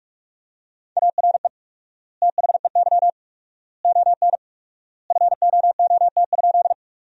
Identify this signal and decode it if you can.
Morse code, heard as a keyed tone, about 46 wpm, 700 Hz. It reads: ARE THEY ON FOOT?
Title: 4x Word-Spacing - Sentences from Top 500 Words - 45wpm